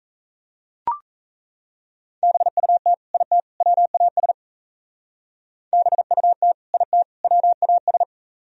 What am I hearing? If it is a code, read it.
Morse code, heard as a keyed tone, about 33 wpm, 700 Hz. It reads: BUT IT WAS BUT IT WAS
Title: Speed-Racing - Sets of 3 Words 28wpm